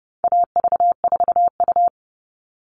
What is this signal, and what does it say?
Morse code, heard as a keyed tone, about 30 wpm, 700 Hz. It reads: AV4U
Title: All Cut Numbers - 4-Digit 30wpm